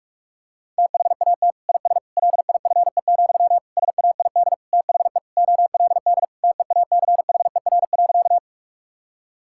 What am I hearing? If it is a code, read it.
Morse code, heard as a keyed tone, about 46 wpm, 700 Hz. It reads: THAT IS LIFE, SAID THE OLD TEACHER.
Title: Sentences from Top 1000 Words - Review Part 6 - 45wpm